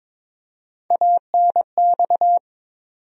Morse code, heard as a keyed tone, about 21 wpm, 700 Hz. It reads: ANX
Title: USA Call Sign Postfixes - 3 Letters 22wpm